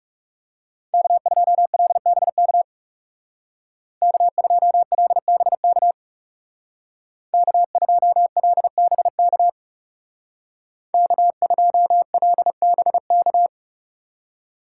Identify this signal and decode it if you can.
Morse code, heard as a keyed tone, about 37 wpm, 700 Hz. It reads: K2LBK K2LBK K2LBK K2LBK
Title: Speed Racing - USA 1x3 Call Signs 30wpm